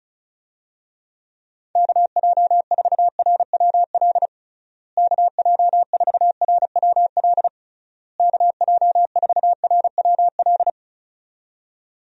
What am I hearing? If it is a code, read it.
Morse code, heard as a keyed tone, about 34 wpm, 700 Hz. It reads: KJ4RWL KJ4RWL KJ4RWL
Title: USA Call Signs - 6 Characters - Repeated 3x - 35wpm